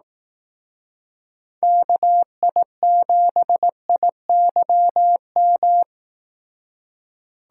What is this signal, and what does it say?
Morse code, heard as a keyed tone, about 17 wpm, 700 Hz. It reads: KI7IYM
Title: Speed Racing - USA Call Signs 15wpm